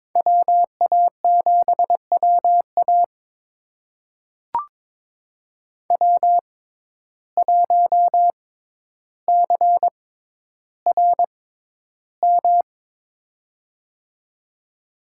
Morse code, heard as a keyed tone, about 22 wpm, spaced as for 11 wpm, 700 Hz. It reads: WA7WA W1CRM
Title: USA Call Signs - 5 Characters - 6x Character Spacing - 22wpm